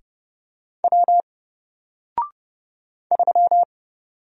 Morse code, heard as a keyed tone, about 30 wpm, 700 Hz. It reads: W 3
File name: Single-Letter-Number-30wpm